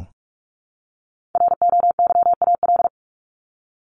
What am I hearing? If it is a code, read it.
Morse code, heard as a keyed tone, about 44 wpm, 700 Hz. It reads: ROYAL
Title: Top 1751-2000 Words 45wpm